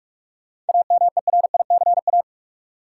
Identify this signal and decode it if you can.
Morse code, heard as a keyed tone, about 46 wpm, 700 Hz. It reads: AMERICA
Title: Sets of 5 Words - New Word Drill - 45wpm